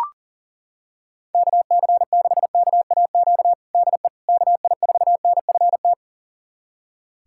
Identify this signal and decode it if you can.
Morse code, heard as a keyed tone, about 40 wpm, 690 Hz. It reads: KC6KAQ DE KI4NFT
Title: USA Callsign DE Callsign - 6 Characters - 40wpm